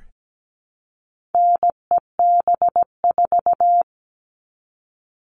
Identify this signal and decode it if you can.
Morse code, heard as a keyed tone, about 17 wpm, 700 Hz. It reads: NEB4
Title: All Cut Numbers - 4-Digit 17wpm